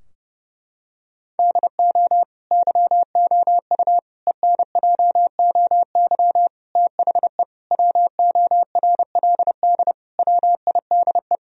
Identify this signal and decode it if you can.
Morse code, heard as a keyed tone, about 30 wpm, 700 Hz. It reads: DO YOU ENJOY THE WORLD WIDE
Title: Sentences from Top 1000 Words - Review Part 4 - 30wpm